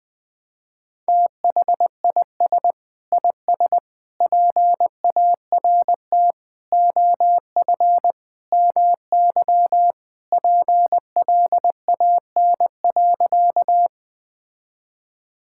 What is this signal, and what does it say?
Morse code, heard as a keyed tone, about 21 wpm, 700 Hz. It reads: THIS IS PART OF MY PLAN.
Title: Sentences with 2-Word Phrases - 20wpm